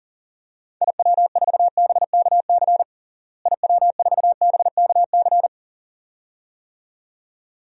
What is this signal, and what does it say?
Morse code, heard as a keyed tone, about 40 wpm, 700 Hz. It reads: IW4BKC IW4BKC
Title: DXCC-100 Call Signs - 6 Characters - Repeated 2x - 40wpm